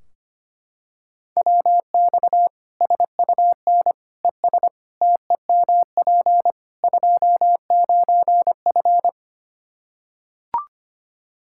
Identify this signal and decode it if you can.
Morse code, heard as a keyed tone, about 25 wpm, 700 Hz. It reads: WX SUN ES TEMP 29F
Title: Practice QSO - Exchange 2 - Info - 25wpm